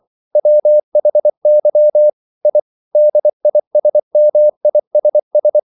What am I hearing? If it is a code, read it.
Morse code, heard as a keyed tone, about 24 wpm, 600 Hz. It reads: WHY I DISMISS
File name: CW / Practice / EnglishQuotations0424